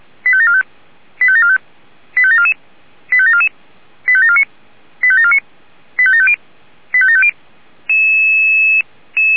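Five-tone selective call (ZVEI), heard as quick runs of five tones, tones of about 70 ms. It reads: 85745 85745 85744 85744 85749 85749 85750 85750
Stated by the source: Five-tone ZVEI selective call received on an SDR.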